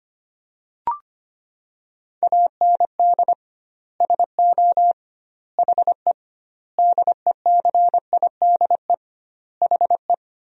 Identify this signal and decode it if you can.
Morse code, heard as a keyed tone, about 25 wpm, 700 Hz. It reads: AND SO HE DECIDE HE
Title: 2x Word-Spacing - Sentences from Top 500 Words - 25wpm